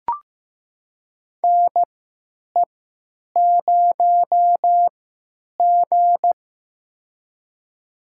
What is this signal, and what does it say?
Morse code, heard as a keyed tone, about 15 wpm, spaced as for 9 wpm, 700 Hz. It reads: NE0G
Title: USA Call Signs - 4 Characters - 3x Character Spacing - 15wpm